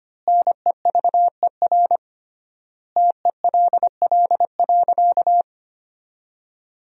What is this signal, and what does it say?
Morse code, heard as a keyed tone, about 25 wpm, 700 Hz. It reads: NEVER TELL.
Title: 3x Word-Spacing - Sentences from Top 300 Words - 25wpm